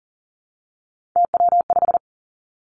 Morse code, heard as a keyed tone, about 43 wpm, 700 Hz. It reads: TW5
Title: Three Challenging Letter-Number Combinations 40wpm